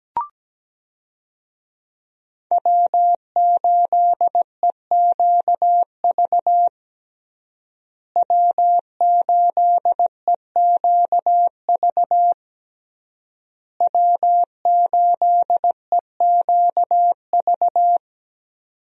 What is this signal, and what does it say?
Morse code, heard as a keyed tone, about 17 wpm, 700 Hz. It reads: W8EQV W8EQV W8EQV
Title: USA Call Signs - 5 Characters - Repeated 3x - 17wpm